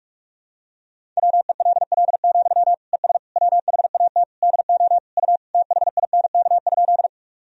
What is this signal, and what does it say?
Morse code, heard as a keyed tone, about 46 wpm, 700 Hz. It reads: WELL, ES WHAT DO U THINK?
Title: Sentences from Top 200 Words and Abbreviations 45wpm